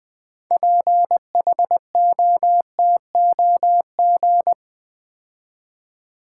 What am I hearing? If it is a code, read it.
Morse code, heard as a keyed tone, about 20 wpm, 700 Hz. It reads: PHOTOG
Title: Build A Word - 6 Letters At A Time - 12-Letter Words - 20wpm